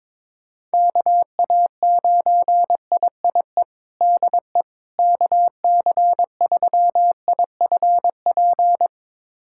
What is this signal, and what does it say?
Morse code, heard as a keyed tone, about 22 wpm, 700 Hz. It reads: KA9IIE DE KC3IFP
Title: USA Callsign DE Callsign 22wpm